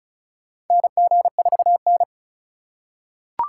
Morse code, heard as a keyed tone, about 36 wpm, 700 Hz. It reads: NG4N
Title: USA Call Signs - 4 Characters - 4x Character Spacing - 35wpm